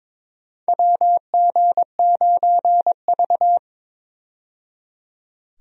Morse code, heard as a keyed tone, about 22 wpm, 700 Hz. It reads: WG9V
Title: USA Call Signs - 4 Characters - 22wpm